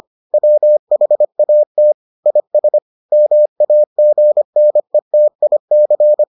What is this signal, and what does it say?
Morse code, heard as a keyed tone, about 25 wpm, 600 Hz. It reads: WHAT IS MAGNETIC